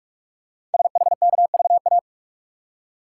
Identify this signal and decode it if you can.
Morse code, heard as a keyed tone, about 44 wpm, 700 Hz. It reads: SHKVA
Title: Code Groups - 5 Letters - 45wpm